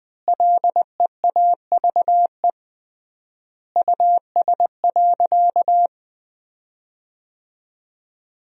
Morse code, heard as a keyed tone, about 19 wpm, 700 Hz. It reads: LEAVE US.
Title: 3x Word-Spacing - Sentences from Top 400 Words - 20wpm